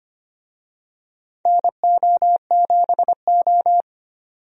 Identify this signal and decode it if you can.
Morse code, heard as a keyed tone, about 25 wpm, 700 Hz. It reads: NO7O
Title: USA Call Signs - 4 Characters - 25wpm